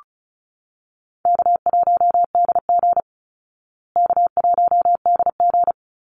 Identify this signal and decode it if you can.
Morse code, heard as a keyed tone, about 34 wpm, 700 Hz. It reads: K1DG K1DG
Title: Easy Call Signs Repeated 3x - 35wpm